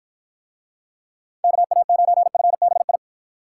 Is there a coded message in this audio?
Yes: Morse code from 1.4 to 3.0 s, heard as a keyed tone, about 49 wpm, 700 Hz. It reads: KA9FBI